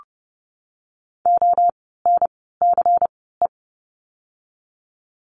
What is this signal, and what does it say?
Morse code, heard as a keyed tone, about 30 wpm, 700 Hz. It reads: O N C E
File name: Lesson 16f - Letter c - Words 30wpm